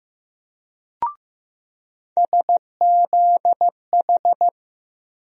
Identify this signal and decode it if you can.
Morse code, heard as a keyed tone, about 15 wpm, 700 Hz. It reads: SZH
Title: USA Call Sign Postfixes - 3 Letters 15wpm